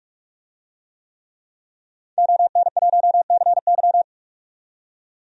Morse code, heard as a keyed tone, about 46 wpm, 700 Hz. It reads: ON1CY
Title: DXCC-100 Call Signs - 5 Characters - 45wpm